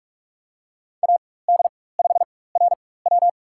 Morse code, heard as a keyed tone, about 46 wpm, spaced as for 21 wpm, 700 Hz. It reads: AD5RW
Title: USA Call Signs - 5 Characters - 4x Character Spacing - 45wpm